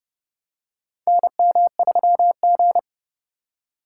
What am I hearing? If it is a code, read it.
Morse code, heard as a keyed tone, about 29 wpm, 700 Hz. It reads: NM3G
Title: USA Call Signs - 4 Characters - 3x Character Spacing - 30wpm